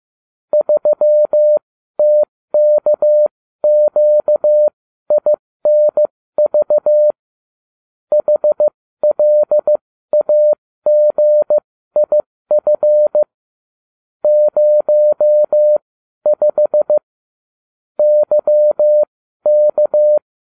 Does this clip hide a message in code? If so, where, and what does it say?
Yes: Morse code from 0.5 to 20.2 s, heard as a keyed tone, about 15 wpm, spaced as for 12 wpm, 600 Hz. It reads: 3TKQINV HLAGIF 05 YK